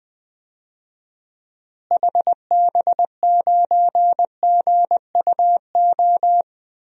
Morse code, heard as a keyed tone, about 20 wpm, 700 Hz. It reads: HB9GUO
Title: DXCC-100 Call Signs - 6 Characters - Repeated 3x - 20wpm